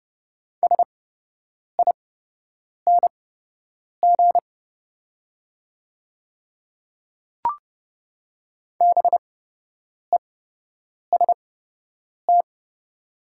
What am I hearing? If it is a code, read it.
Morse code, heard as a keyed tone, about 30 wpm, spaced as for 8 wpm, 700 Hz. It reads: SING BEST
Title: 8x Character Spacing - Top 500 Words - 30wpm